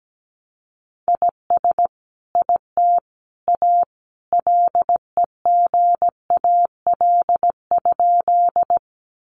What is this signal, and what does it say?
Morse code, heard as a keyed tone, about 17 wpm, 700 Hz. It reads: IS IT A LEGAL?
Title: Sentences from Top 1200 Words 17wpm